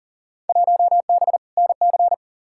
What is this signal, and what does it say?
Morse code, heard as a keyed tone, about 40 wpm, 700 Hz. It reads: 1B NC
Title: Field Day Exchanges Encore - 40wpm